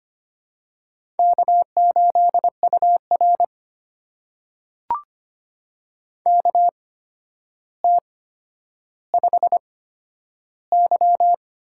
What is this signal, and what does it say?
Morse code, heard as a keyed tone, about 25 wpm, spaced as for 11 wpm, 700 Hz. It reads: K8UR KT5Y
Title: USA Call Signs - 4 Characters - 8x Character Spacing - 25wpm